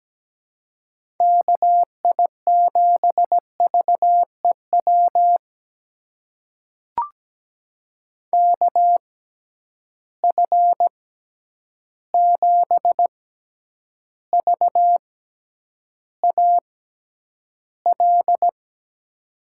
Morse code, heard as a keyed tone, about 17 wpm, spaced as for 9 wpm, 700 Hz. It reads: KI7VEW KF7VAL
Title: USA Call Signs - 6 Characters - 6x Character Spacing - 17wpm